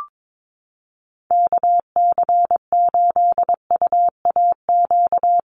Morse code, heard as a keyed tone, about 22 wpm, 700 Hz. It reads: KC8UAQ
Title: USA Call Signs Repeated 3x 22wpm